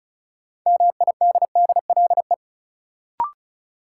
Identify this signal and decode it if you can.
Morse code, heard as a keyed tone, about 34 wpm, 700 Hz. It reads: MIDDLE
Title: Sets of 2 Words - New Word Drill - 35wpm